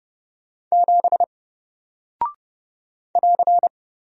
Morse code, heard as a keyed tone, about 30 wpm, 700 Hz. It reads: 7 <AR>
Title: Single Character (Letter:Number:Punctuation:Prosign) 30wpm